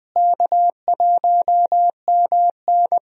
Morse code, heard as a keyed tone, about 21 wpm, 700 Hz. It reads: K1MN
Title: USA Call Signs - 4 Characters - Repeated 3x - 20wpm